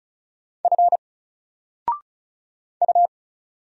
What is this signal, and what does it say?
Morse code, heard as a keyed tone, about 36 wpm, 700 Hz. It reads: F U
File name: Single-Letter-Number-35wpm